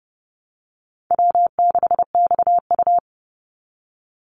Morse code, heard as a keyed tone, about 30 wpm, 700 Hz. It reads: W6XU
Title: Call Signs Repeated 3x - 30wpm